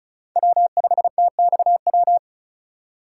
Morse code, heard as a keyed tone, about 36 wpm, 700 Hz. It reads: W5TXW
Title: USA Call Signs - 3x Character Spacing - 35wpm